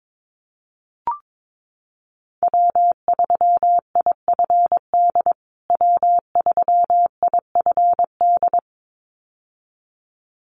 Morse code, heard as a keyed tone, about 22 wpm, 700 Hz. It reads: W3IFD W3IFD
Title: USA Call Signs Repeated 2x 22wpm